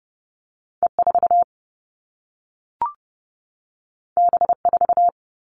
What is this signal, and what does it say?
Morse code, heard as a keyed tone, about 30 wpm, 700 Hz. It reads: E4 B4
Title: All Cut Numbers - 2-Digit 30wpm